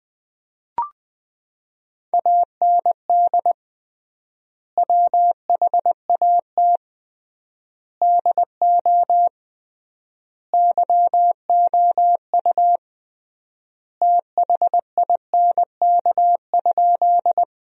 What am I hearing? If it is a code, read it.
Morse code, heard as a keyed tone, about 20 wpm, 700 Hz. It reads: AND WHAT DO YOU THINK?
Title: 3x Word-Spacing - Sentences from Top 200 Words - 20wpm